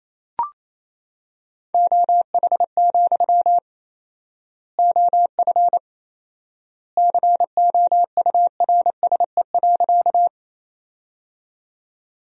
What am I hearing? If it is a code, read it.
Morse code, heard as a keyed tone, about 28 wpm, 700 Hz. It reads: OH, OF COURSE.
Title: 4x Word-Spacing - Sentences from Top 500 Words - 28wpm